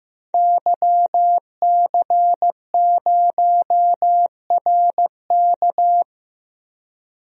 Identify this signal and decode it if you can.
Morse code, heard as a keyed tone, about 15 wpm, 700 Hz. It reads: YC0RK
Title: DXCC-100 Call Signs - 5 Characters - 15wpm